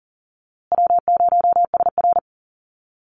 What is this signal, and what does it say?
Morse code, heard as a keyed tone, about 40 wpm, 700 Hz. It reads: W0SR
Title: USA Call Signs Repeated 3x 40wpm